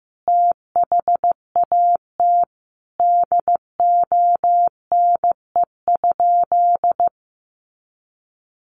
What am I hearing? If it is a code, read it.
Morse code, heard as a keyed tone, about 15 wpm, 700 Hz. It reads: THAT DONE?
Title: Sentences from Top 500 Words 15wpm